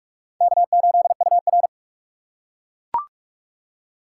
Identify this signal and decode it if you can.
Morse code, heard as a keyed tone, about 44 wpm, 700 Hz. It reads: K8UR
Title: USA Call Signs - 4x Character Spacing - 45wpm